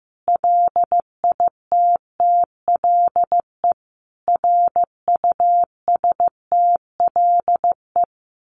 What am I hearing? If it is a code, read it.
Morse code, heard as a keyed tone, about 15 wpm, 700 Hz. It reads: LITTLE RUSTLE
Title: A Sound Of Thunder 15wpm